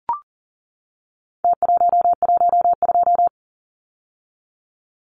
Morse code, heard as a keyed tone, about 40 wpm, 700 Hz. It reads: T112
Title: Numbers - Contest Style - 4-Digit 40wpm